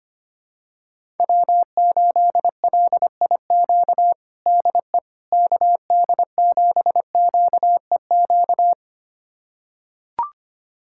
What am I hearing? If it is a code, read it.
Morse code, heard as a keyed tone, about 25 wpm, 700 Hz. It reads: W8LIQ DE KD7QEQ
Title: USA Callsign DE Callsign 25wpm